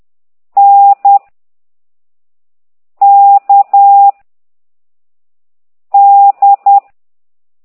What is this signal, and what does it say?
Morse code, heard as a keyed tone, about 10 wpm, 800 Hz. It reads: N K D
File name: exerc7